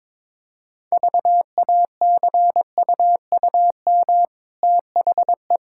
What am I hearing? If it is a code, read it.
Morse code, heard as a keyed tone, about 22 wpm, 690 Hz. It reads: VACUUM THE
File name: practice / 22wpm / Easy Sentences with Hints - 22wpm